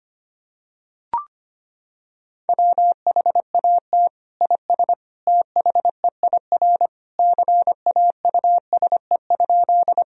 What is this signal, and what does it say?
Morse code, heard as a keyed tone, about 25 wpm, 700 Hz. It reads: WHAT IS THEIR CAUSE?
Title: Sentences from Top 1000 Words - Review Part 1 - 25wpm